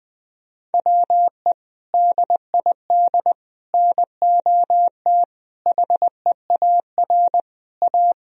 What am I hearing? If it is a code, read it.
Morse code, heard as a keyed tone, about 20 wpm, 700 Hz. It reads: WE DID NOT HEAR A
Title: Sentences from Top 1000 Words - Review Part 4 - 20wpm